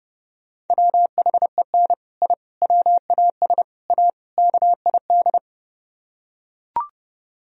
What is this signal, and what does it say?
Morse code, heard as a keyed tone, about 32 wpm, 700 Hz. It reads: WHEN I WAS A KID